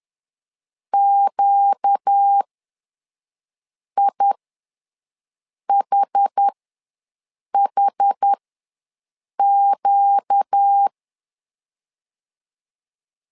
Morse code, heard as a keyed tone, about 10 wpm, spaced as for 5 wpm, 800 Hz. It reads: QIHHQ
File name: LEZIONI / 4 Q ws 10wpm 2306